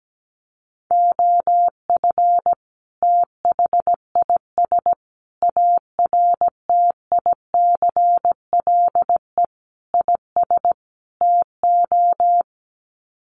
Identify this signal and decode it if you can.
Morse code, heard as a keyed tone, about 17 wpm, 700 Hz. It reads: OF THIS ARTICLE IS TO
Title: Sets of 5 Words Encore2 17wpm